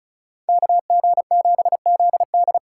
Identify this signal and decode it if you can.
Morse code, heard as a keyed tone, about 43 wpm, 700 Hz. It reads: KG7ZD
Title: Speed Racing - USA Call Signs 35wpm